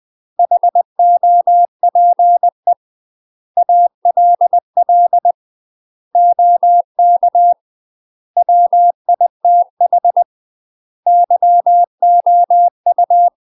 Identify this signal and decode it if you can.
Morse code, heard as a keyed tone, about 20 wpm, 700 Hz. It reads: HOPE ALL OK WITH YOU
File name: UK MASTER QSOs TRAINING/20WPM